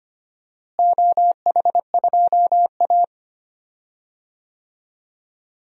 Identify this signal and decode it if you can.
Morse code, heard as a keyed tone, about 25 wpm, 700 Hz. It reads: OH2A
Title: DXCC-100 Call Signs - 4 Characters - Repeated 3x - 25wpm